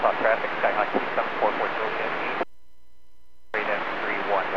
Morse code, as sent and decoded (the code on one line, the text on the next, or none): none